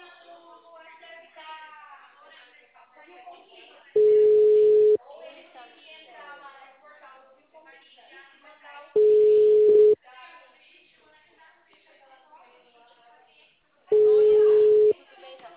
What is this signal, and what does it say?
Ringback tone (Europe, 425 Hz), with speech.